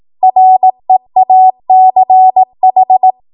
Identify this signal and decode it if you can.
Morse code, heard as a keyed tone, about 18 wpm, 750 Hz. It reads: REACH
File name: Morse / 18WPM